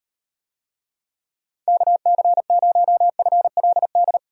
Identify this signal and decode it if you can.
Morse code, heard as a keyed tone, about 35 wpm, 700 Hz. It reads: KC0FLD